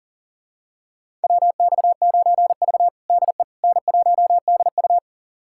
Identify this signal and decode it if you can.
Morse code, heard as a keyed tone, about 40 wpm, 700 Hz. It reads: WX9V DE N1DU
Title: USA Callsign DE Callsign - 4 Characters - 40wpm